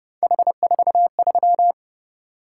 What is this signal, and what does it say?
Morse code, heard as a keyed tone, about 29 wpm, 700 Hz. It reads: H43
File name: Speed-Racing - Three Letter-Number Combinations - 25wpm